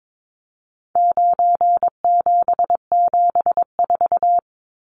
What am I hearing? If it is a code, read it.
Morse code, heard as a keyed tone, about 22 wpm, 700 Hz. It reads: 9774
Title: Numbers - 4-Digit 22wpm